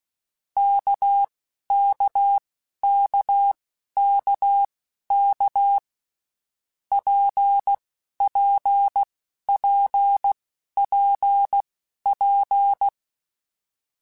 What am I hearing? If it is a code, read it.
Morse code, heard as a keyed tone, about 16 wpm, spaced as for 11 wpm, 800 Hz. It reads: KKKKK PPPPP
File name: Basic Letter Groups By7Groups / RKPX / RKPX@10